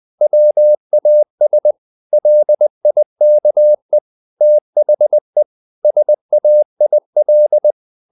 Morse code, heard as a keyed tone, about 20 wpm, 600 Hz. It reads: WAS LIKE THE SAIL
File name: the old man and the sea / the old man and the sea part 1 20wpm